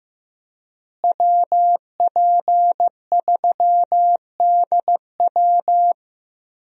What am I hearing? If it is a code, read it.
Morse code, heard as a keyed tone, about 15 wpm, 700 Hz. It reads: WP3DW
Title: USA Call Signs - 5 Characters - Repeated 2x - 15wpm